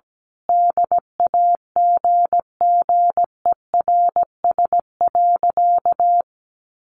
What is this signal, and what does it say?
Morse code, heard as a keyed tone, about 17 wpm, 700 Hz. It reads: DAGGERS.
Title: A Sound Of Thunder 17wpm